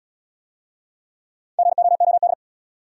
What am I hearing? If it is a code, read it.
Morse code, heard as a keyed tone, about 69 wpm, 700 Hz. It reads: GOOD